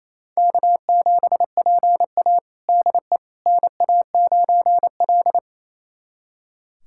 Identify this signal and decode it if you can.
Morse code, heard as a keyed tone, about 28 wpm, 700 Hz. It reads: K7PA DE NA9L
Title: USA Callsign DE Callsign - 4 Characters - 28wpm